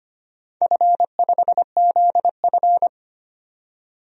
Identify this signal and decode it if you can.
Morse code, heard as a keyed tone, about 25 wpm, 700 Hz. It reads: F5ZF
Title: DXCC-100 Call Signs - 4 Characters - 25wpm